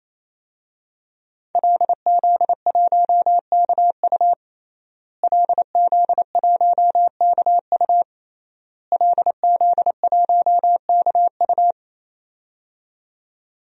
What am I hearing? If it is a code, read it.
Morse code, heard as a keyed tone, about 27 wpm, 700 Hz. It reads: LZ1KU LZ1KU LZ1KU
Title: DXCC-100 Call Signs - 5 Characters - Repeated 3x - 28wpm